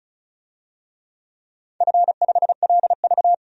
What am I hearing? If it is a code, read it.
Morse code, heard as a keyed tone, about 36 wpm, 700 Hz. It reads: F5LV
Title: DXCC-100 Call Signs - 4 Characters - 35wpm